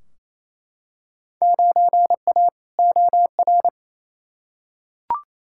Morse code, heard as a keyed tone, about 28 wpm, 700 Hz. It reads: 9A OR
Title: Field Day Exchanges Encore - 28wpm